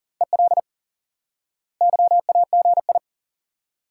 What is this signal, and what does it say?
Morse code, heard as a keyed tone, about 40 wpm, 700 Hz. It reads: EL YAGI
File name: IQE Course 3 - Lesson 89 - YAGI - Two Elements - 40wpm